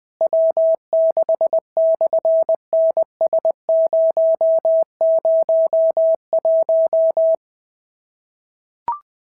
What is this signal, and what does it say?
Morse code, heard as a keyed tone, about 20 wpm, 650 Hz. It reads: W6/NS001
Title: SOTA QSO Practice 20wpm